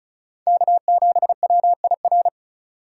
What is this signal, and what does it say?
Morse code, heard as a keyed tone, about 44 wpm, 700 Hz. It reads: K7WIR